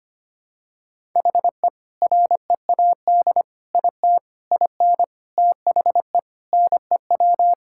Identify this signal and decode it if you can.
Morse code, heard as a keyed tone, about 25 wpm, 700 Hz. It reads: HE READ IT IN THE NEW
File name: practice / 25wpm / Sentences with 4-Word Phrases - 25wpm